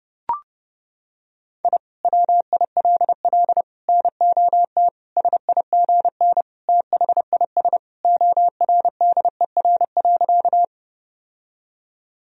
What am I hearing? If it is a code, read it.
Morse code, heard as a keyed tone, about 30 wpm, 700 Hz. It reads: I WILL NOT SIGN THIS ORDER.
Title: Sentences from Top 1000 Words - Review Part 2 - 30wpm